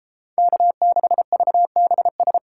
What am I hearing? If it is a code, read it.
Morse code, heard as a keyed tone, about 31 wpm, 700 Hz. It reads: K6VBS